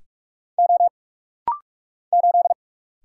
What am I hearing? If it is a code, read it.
Morse code, heard as a keyed tone, about 44 wpm, 700 Hz. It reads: O 8
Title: Single Letter-Number Alt - Rapid-Fire 45wpm